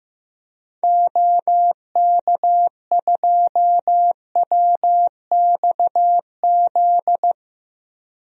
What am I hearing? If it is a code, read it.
Morse code, heard as a keyed tone, about 15 wpm, 700 Hz. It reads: OK2WXZ
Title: DXCC-100 Call Signs - 6 Characters - 15wpm